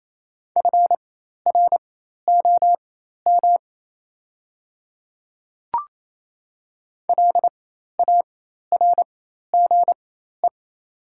Morse code, heard as a keyed tone, about 28 wpm, spaced as for 13 wpm, 700 Hz. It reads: FROM LARGE
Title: 4x Character Spacing - Top 500 Words - 28wpm